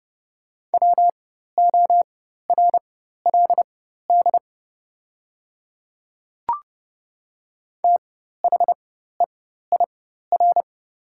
Morse code, heard as a keyed tone, about 31 wpm, spaced as for 14 wpm, 700 Hz. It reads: WORLD THEIR
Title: 4x Character Spacing - Top 200 Words - 30wpm